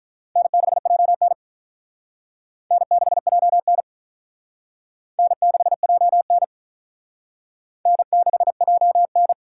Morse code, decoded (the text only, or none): N6JN N6JN N6JN N6JN